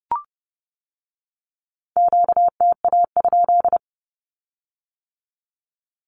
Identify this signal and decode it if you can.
Morse code, heard as a keyed tone, about 30 wpm, 700 Hz. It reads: QTA?
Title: Q-Codes with Spoken Meaning - 30wpm